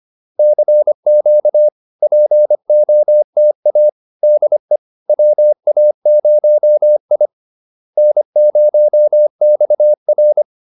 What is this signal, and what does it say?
Morse code, heard as a keyed tone, about 25 wpm, 600 Hz. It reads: CQ POTA DE WA0I N0XR
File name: POTA107 25